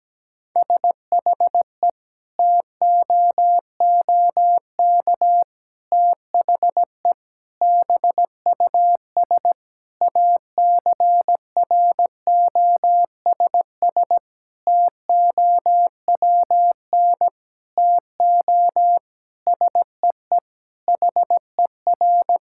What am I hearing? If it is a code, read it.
Morse code, heard as a keyed tone, about 17 wpm, 700 Hz. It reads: SHE TOOK THE BUS ACROSS TOWN TO SEE HER
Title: Sentences from Top 1000 Words - Review Part 6 - 17wpm